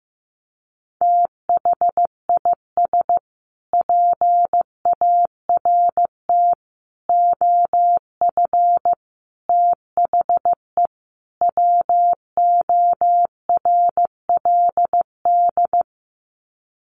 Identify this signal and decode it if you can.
Morse code, heard as a keyed tone, about 15 wpm, 700 Hz. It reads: THIS PART OF THE WORLD
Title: Sets of 5 Words Encore2 15wpm